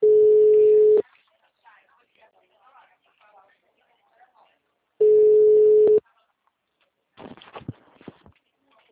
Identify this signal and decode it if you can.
Ringback tone (Europe, 425 Hz), with speech.